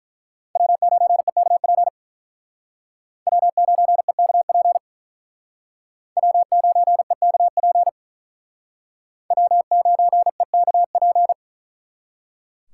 Morse code, heard as a keyed tone, about 42 wpm, 700 Hz. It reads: W9EKP W9EKP W9EKP W9EKP
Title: Speed Racing - USA 1x3 Call Signs 35wpm